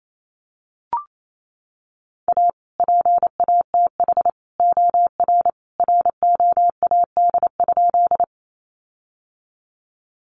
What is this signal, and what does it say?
Morse code, heard as a keyed tone, about 28 wpm, 700 Hz. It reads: A PATH OR ROAD?
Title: Sentences from Top 1400 Words 28wpm